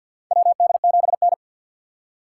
Morse code, heard as a keyed tone, about 50 wpm, 700 Hz. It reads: WD7N
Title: USA Call Signs - 4 Characters - 3x Character Spacing - 50wpm